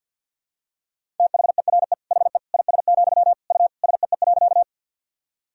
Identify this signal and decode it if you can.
Morse code, heard as a keyed tone, about 50 wpm, 700 Hz. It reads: THERE HE IS, U SEE.
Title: Sentences from Top 100 Words and Abbreviations 50wpm